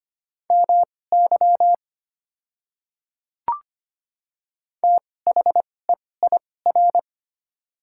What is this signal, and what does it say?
Morse code, heard as a keyed tone, about 25 wpm, spaced as for 18 wpm, 690 Hz. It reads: MY THEIR
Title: 2x Character Spacing - Top 100 Words - 25wpm